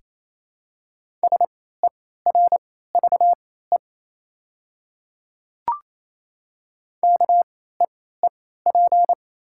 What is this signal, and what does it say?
Morse code, heard as a keyed tone, about 28 wpm, spaced as for 16 wpm, 690 Hz. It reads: SERVE KEEP
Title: 3x Character Spacing - Top 500 Words - 28wpm